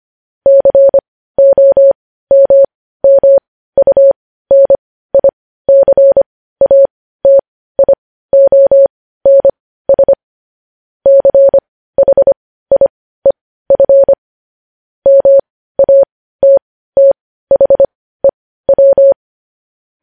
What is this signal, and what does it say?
Morse code, heard as a keyed tone, about 25 wpm, spaced as for 15 wpm, 550 Hz. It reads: COMMUNICATIONS CHIEF MATTHEW
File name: podcast / News headlines-1x-25x15